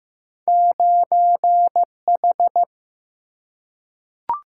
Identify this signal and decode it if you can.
Morse code, heard as a keyed tone, about 15 wpm, 700 Hz. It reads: 9H
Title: Single-Letter Single-Number 15wpm